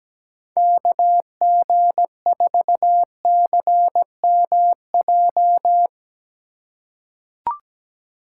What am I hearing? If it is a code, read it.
Morse code, heard as a keyed tone, about 17 wpm, 700 Hz. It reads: KG4CMJ
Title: USA Call Signs - 6 Characters - 4x Character Spacing - 17wpm